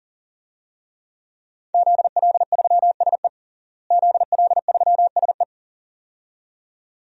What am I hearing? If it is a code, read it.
Morse code, heard as a keyed tone, about 40 wpm, 700 Hz. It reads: ZL3SE ZL3SE
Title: DXCC-100 Call Signs - 5 Characters - Repeated 2x - 40wpm